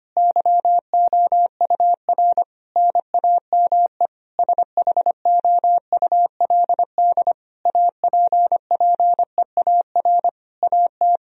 Morse code, heard as a keyed tone, about 25 wpm, 700 Hz. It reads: YOUR NAME SHOULD APPEAR AT
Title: Sentences from Top 1000 Words - Review Part 2 - 25wpm